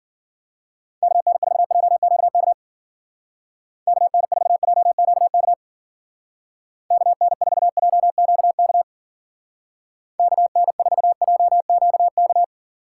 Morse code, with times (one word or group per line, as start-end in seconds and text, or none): 1.0-2.5 KN4JQK
3.9-5.5 KN4JQK
6.9-8.8 KN4JQK
10.2-12.4 KN4JQK